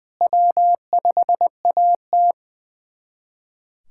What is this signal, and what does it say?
Morse code, heard as a keyed tone, about 20 wpm, 700 Hz. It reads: W5AT
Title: USA Callsign DE Callsign - 4 Characters - 20wpm